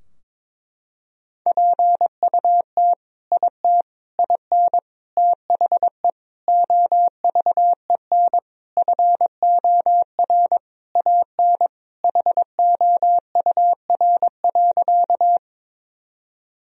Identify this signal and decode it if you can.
Morse code, heard as a keyed tone, about 21 wpm, 700 Hz. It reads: PUT IT IN THE OVEN FOR AN HOUR.
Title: Sentences from Top 1000 Words - Review Part 4 - 22wpm